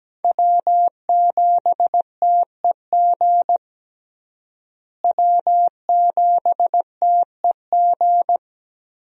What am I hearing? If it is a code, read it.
Morse code, heard as a keyed tone, about 17 wpm, 700 Hz. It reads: W7TEG W7TEG
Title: USA Call Signs - 5 Characters - Repeated 3x - 17wpm